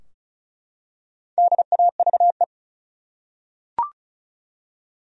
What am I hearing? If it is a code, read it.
Morse code, heard as a keyed tone, about 34 wpm, 700 Hz. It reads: DAVE
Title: USA Common Names - 35wpm